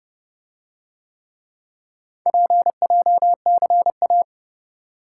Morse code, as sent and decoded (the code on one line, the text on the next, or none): .--. .--- -.-. .-
PJCA